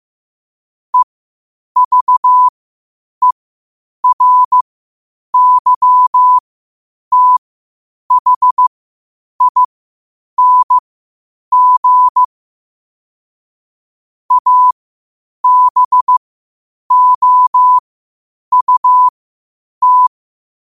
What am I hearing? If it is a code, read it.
Morse code, heard as a keyed tone, about 15 wpm, spaced as for 8 wpm, 1000 Hz. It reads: EVERYTHING ABOUT
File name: qotdfiles / 8wpm